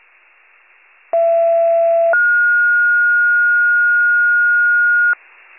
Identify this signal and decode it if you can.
Two-tone page, heard as a one-second tone, then a three-second tone, about 670 then 1430 Hz. It is heard from 1.1 to 5.1 s.